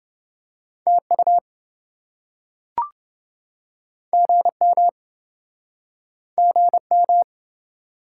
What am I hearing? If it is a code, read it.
Morse code, heard as a keyed tone, about 27 wpm, 700 Hz. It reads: TU GM GM